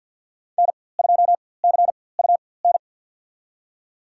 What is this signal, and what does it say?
Morse code, heard as a keyed tone, about 46 wpm, spaced as for 23 wpm, 700 Hz. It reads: N2CUN
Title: USA Call Signs - 5 Characters - 4x Character Spacing - 50wpm